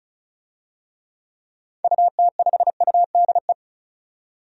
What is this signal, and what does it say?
Morse code, heard as a keyed tone, about 36 wpm, 700 Hz. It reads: UT5UDE
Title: DXCC-100 Call Signs - 6 Characters - 35wpm